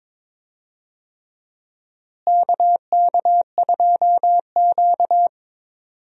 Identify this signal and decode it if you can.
Morse code, heard as a keyed tone, about 22 wpm, 700 Hz. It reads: KK2Q
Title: USA Call Signs - 4 Characters - Repeated 3x - 22wpm